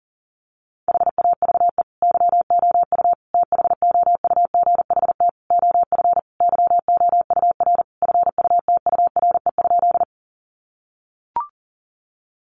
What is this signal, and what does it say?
Morse code, heard as a keyed tone, about 40 wpm, 700 Hz. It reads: HAVE YOU THOUGHT OF YOUR FUTURE?
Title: Sentences from Top 700 Words 40wpm